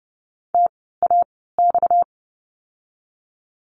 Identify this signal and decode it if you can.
Morse code, heard as a keyed tone, about 30 wpm, spaced as for 17 wpm, 700 Hz. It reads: TAX
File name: course / 30:10wpm / 230 Lesson 37f - Letter x - Words 30wpm